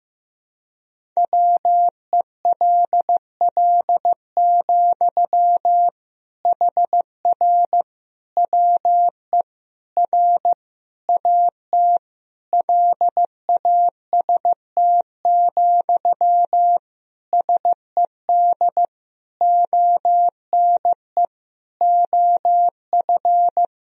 Morse code, heard as a keyed tone, about 15 wpm, 700 Hz. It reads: WELL, HR WE R AT LAST, SED ONE OF
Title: Sentences from Top 300 Words and Abbreviations 15wpm